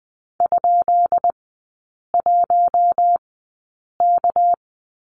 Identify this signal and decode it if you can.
Morse code, heard as a keyed tone, about 20 wpm, spaced as for 8 wpm, 700 Hz. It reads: ?1K
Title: 198 Lesson 32e - Letter k - Three Characters 20wpm